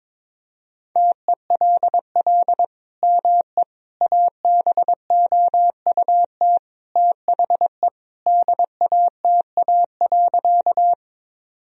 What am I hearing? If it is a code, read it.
Morse code, heard as a keyed tone, about 22 wpm, 700 Hz. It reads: TELL ME ABOUT THE DATA.
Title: Sentences from Top 1000 Words - Review Part 3 - 22wpm